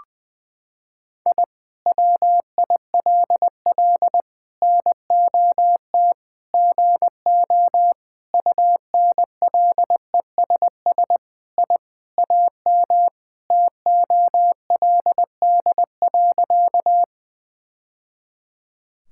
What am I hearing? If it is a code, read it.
Morse code, heard as a keyed tone, about 20 wpm, 700 Hz. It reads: I WILL NOT GO UNLESS I AM TOLD.
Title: Sentences from Top 1000 Words - Review Part 6 - 20wpm